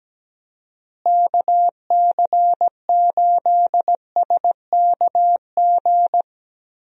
Morse code, heard as a keyed tone, about 17 wpm, 700 Hz. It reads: KC8SKG
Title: USA Call Signs - 6 Characters - Repeated 2x - 17wpm